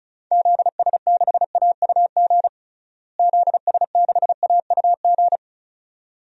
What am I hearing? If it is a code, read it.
Morse code, heard as a keyed tone, about 34 wpm, 700 Hz. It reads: ZS6AUG ZS6AUG
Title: DXCC-100 Call Signs - 6 Characters - Repeated 2x - 35wpm